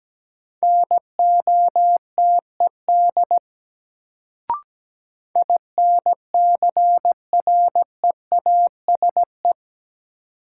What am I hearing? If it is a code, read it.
Morse code, heard as a keyed tone, about 17 wpm, 700 Hz. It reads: NOTED INCREASE
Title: Sets of 5 Words - New Word Drill - 17wpm